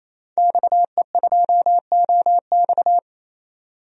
Morse code, heard as a keyed tone, about 28 wpm, 700 Hz. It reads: XE2OX
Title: DXCC-100 Call Signs - 5 Characters - 28wpm